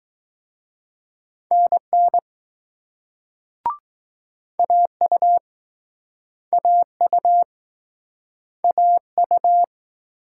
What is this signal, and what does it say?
Morse code, heard as a keyed tone, about 20 wpm, 700 Hz. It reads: NN AU AU AU